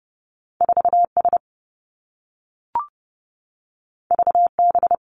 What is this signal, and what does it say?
Morse code, heard as a keyed tone, about 30 wpm, 700 Hz. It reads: 4 S V B
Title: Two Challenging Letter-Number Combinations 30wpm